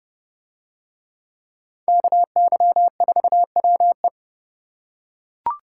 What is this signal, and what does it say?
Morse code, heard as a keyed tone, about 30 wpm, 700 Hz. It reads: KY4WE